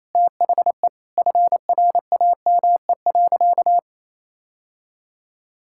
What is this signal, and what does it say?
Morse code, heard as a keyed tone, about 28 wpm, 700 Hz. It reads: THE FRAME.
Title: Sentences from Top 1000 Words - Review Part 6 - 28wpm